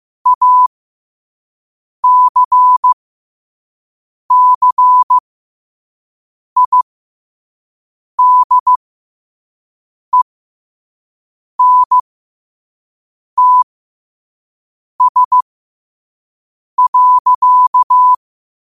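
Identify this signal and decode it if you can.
Morse code, heard as a keyed tone, about 15 wpm, spaced as for 5 wpm, 1000 Hz. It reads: ACCIDENTS.
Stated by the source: Morse code sent at 5 wpm.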